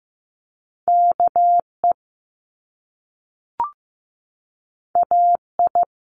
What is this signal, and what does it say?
Morse code, heard as a keyed tone, about 15 wpm, 700 Hz. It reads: KE AI
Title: Top Two Letter Combinations in Words 15wpm